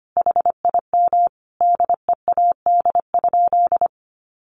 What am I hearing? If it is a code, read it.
Morse code, heard as a keyed tone, about 25 wpm, 700 Hz. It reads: HIM DEAD?
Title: Sentences from Top 1500 Words 25wpm